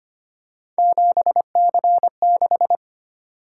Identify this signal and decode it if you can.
Morse code, heard as a keyed tone, about 25 wpm, 700 Hz. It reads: 7C6
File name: Three Letter-Number Combinations - 25wpm